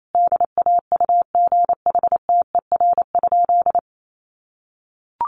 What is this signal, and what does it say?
Morse code, heard as a keyed tone, about 28 wpm, 700 Hz. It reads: DAUGHTER?
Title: Sentences from Top 1500 Words 28wpm